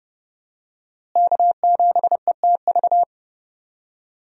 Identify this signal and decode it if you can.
Morse code, heard as a keyed tone, about 30 wpm, 700 Hz. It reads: K7ETV
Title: USA Call Signs - 5 Characters - Repeated 3x - 30wpm